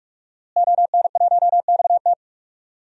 Morse code, heard as a keyed tone, about 47 wpm, 700 Hz. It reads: ON1XT
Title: DXCC-100 Call Signs - 5 Characters - Repeated 2x - 45wpm